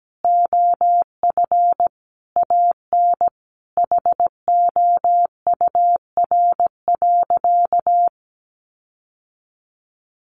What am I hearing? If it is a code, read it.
Morse code, heard as a keyed tone, about 17 wpm, 700 Hz. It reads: OF AN HOUR.
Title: Sentences from Top 1500 Words 17wpm